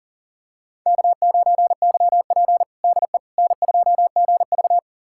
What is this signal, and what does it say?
Morse code, heard as a keyed tone, about 40 wpm, 700 Hz. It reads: K9YP DE N2GV
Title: USA Callsign DE Callsign - 4 Characters - 40wpm